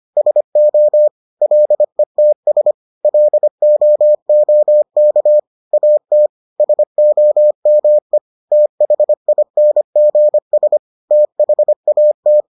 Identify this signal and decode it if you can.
Morse code, heard as a keyed tone, about 25 wpm, 600 Hz. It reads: SO LETS LOOK AT SOME THINGS THAT